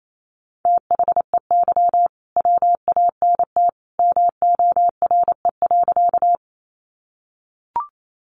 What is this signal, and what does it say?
Morse code, heard as a keyed tone, about 28 wpm, 700 Hz. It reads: THEY WANT MORE.
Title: Sentences from Top 200 Words 28wpm